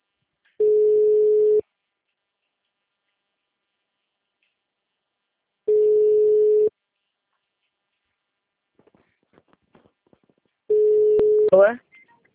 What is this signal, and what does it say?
Ringback tone (Europe, 425 Hz), with speech.
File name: out- -2014- - - .40